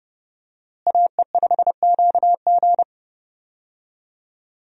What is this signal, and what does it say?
Morse code, heard as a keyed tone, about 30 wpm, 700 Hz. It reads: AE5QG
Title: USA Call Signs - 5 Characters - Repeated 3x - 30wpm